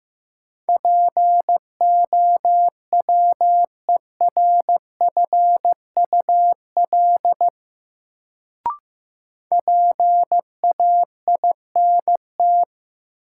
Morse code, heard as a keyed tone, about 15 wpm, 700 Hz. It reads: POWERFUL PAINT
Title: Sentences from Top 1000 Words - Review Part 6 - New Word Drill - 15wpm